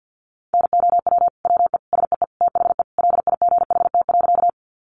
Morse code, heard as a keyed tone, about 50 wpm, 700 Hz. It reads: NOW WE SEE THE LIGHT.
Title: Sentences from Top 200 Words 50wpm